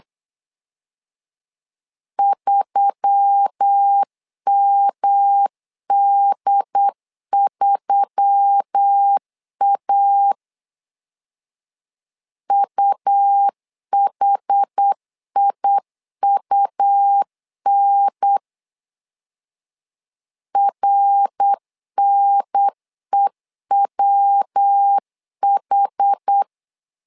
Morse code, heard as a keyed tone, about 8 wpm, 800 Hz. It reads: 3MD3A UHIUN RNEWH
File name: LEZIONI / 4 Q1S3 8wpm 3037